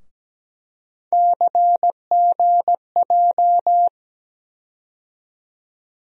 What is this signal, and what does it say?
Morse code, heard as a keyed tone, about 17 wpm, 700 Hz. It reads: CGJ
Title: USA Call Sign Postfixes - 3 Letters 17wpm